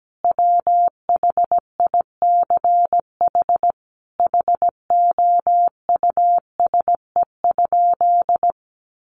Morse code, heard as a keyed tone, about 17 wpm, 700 Hz. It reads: WHICH HOUSE?
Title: Sentences from Top 1100 Words 17wpm